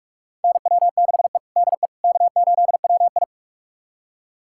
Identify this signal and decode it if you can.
Morse code, heard as a keyed tone, about 46 wpm, 700 Hz. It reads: NW6E DE K8WI
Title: USA Callsign DE Callsign - 4 Characters - 45wpm